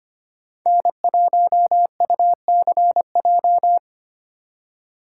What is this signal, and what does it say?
Morse code, heard as a keyed tone, about 25 wpm, 700 Hz. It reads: N1UCJ
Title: USA Call Signs - 5 Characters - 4x Character Spacing - 25wpm